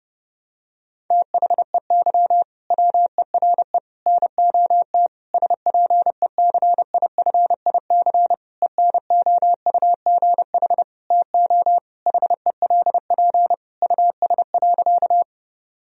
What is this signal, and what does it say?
Morse code, heard as a keyed tone, about 30 wpm, 700 Hz. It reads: THEY WERE NOT SPECIFIC ENOUGH TO HELP US.
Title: Sentences from Top 1000 Words - Review Part 3 - 30wpm